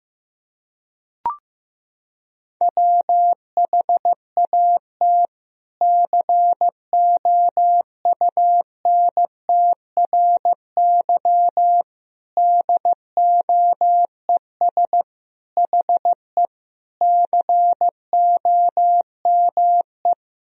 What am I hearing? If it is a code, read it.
Morse code, heard as a keyed tone, about 16 wpm, 700 Hz. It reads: WHAT COUNTRY DOES HE COME
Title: Sentences from Top 300 Words and Abbreviations 15wpm